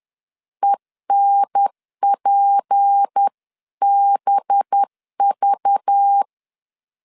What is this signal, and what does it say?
Morse code, heard as a keyed tone, about 10 wpm, 800 Hz. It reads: ENPBV
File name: LEZIONI / 7 GF9Z / 7 G 10wpm 2501